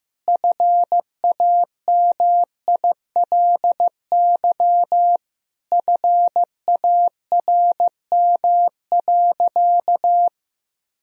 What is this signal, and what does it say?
Morse code, heard as a keyed tone, about 16 wpm, 700 Hz. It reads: FAMILY FARM.
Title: Sentences from Top 1000 Words - Review Part 2 - 15wpm